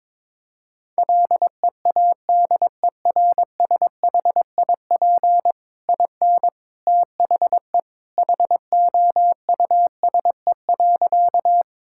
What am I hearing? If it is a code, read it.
Morse code, heard as a keyed tone, about 22 wpm, 700 Hz. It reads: LEADERSHIP IN THE HOUSE.
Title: Sentences from Top 1600 Words 22wpm